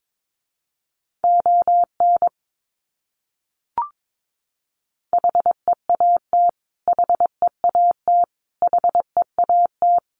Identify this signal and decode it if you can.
Morse code, heard as a keyed tone, about 22 wpm, 700 Hz. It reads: ON HEAT HEAT HEAT
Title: Top 500 Words Repeated 22wpm